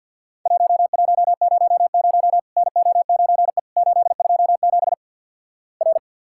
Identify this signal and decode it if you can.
Morse code, heard as a keyed tone, about 50 wpm, 700 Hz. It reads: 1100 NO9E 827 R
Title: Worked All Europe DX Contest - QTC Exchange with Repeats - 50wpm